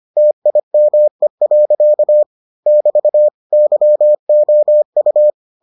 Morse code, heard as a keyed tone, about 25 wpm, 600 Hz. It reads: TIME. <BT> YOU